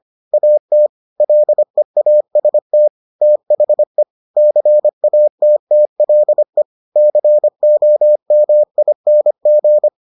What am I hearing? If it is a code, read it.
Morse code, heard as a keyed tone, about 25 wpm, 600 Hz. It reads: AT LEAST THE CATTLE COMING